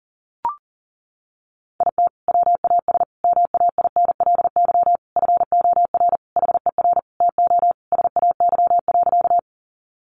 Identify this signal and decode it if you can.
Morse code, heard as a keyed tone, about 40 wpm, 700 Hz. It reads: IT WAS MAINLY FOR HER TO SAY.
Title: Sentences from Top 1400 Words 40wpm